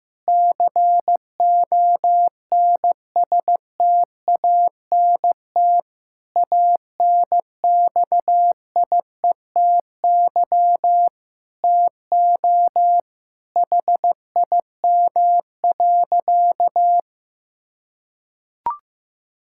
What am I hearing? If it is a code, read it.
Morse code, heard as a keyed tone, about 16 wpm, 700 Hz. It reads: CONSTANT ANXIETY TO HIM.
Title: Sentences from Top 1600 Words 15wpm